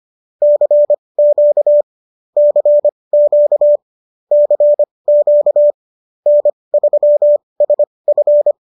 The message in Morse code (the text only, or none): CQ CQ CQ N3SF